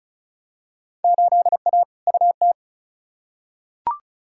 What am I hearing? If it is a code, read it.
Morse code, heard as a keyed tone, about 34 wpm, 700 Hz. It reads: 8A UT
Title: Field Day Exchanges Encore - 35wpm